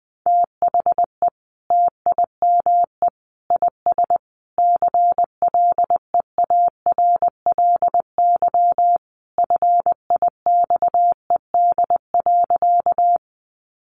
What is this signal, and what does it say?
Morse code, heard as a keyed tone, about 20 wpm, 700 Hz. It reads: THE TIME IS CLEARLY FIXED.
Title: Sentences from Top 1300 Words 20wpm